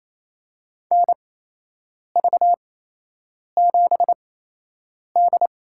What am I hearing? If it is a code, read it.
Morse code, heard as a keyed tone, about 28 wpm, spaced as for 8 wpm, 700 Hz. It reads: NV7D
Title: USA Call Signs - 8x Character Spacing - 28wpm